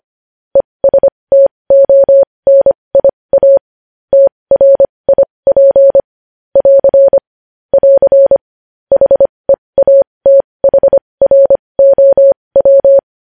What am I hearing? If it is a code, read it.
Morse code, heard as a keyed tone, about 25 wpm, spaced as for 20 wpm, 550 Hz. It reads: ESTONIA TRIP <AR> <AR> HEATHROW
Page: news-headlines-in-morse-code-at-20-wpm